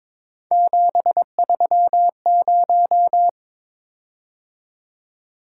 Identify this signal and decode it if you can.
Morse code, heard as a keyed tone, about 22 wpm, 700 Hz. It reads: 730
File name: practice / 22wpm / Three Letter-Number Combinations - 22wpm